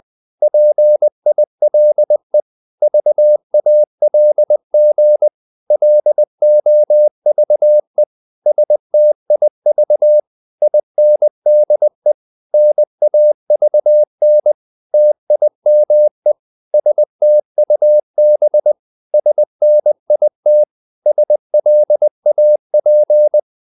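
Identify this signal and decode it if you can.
Morse code, heard as a keyed tone, about 20 wpm, 600 Hz. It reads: PILE VALG LOVE STIV INDE NAVN TIME STUB SNIT SLAP
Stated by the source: Morse code practice of short Danish words at 20 wpm, from CW med Gnister.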